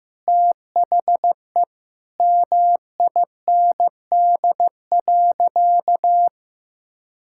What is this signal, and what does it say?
Morse code, heard as a keyed tone, about 15 wpm, 700 Hz. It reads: THE MIND.
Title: Sentences from Top 1000 Words - Review Part 6 - 15wpm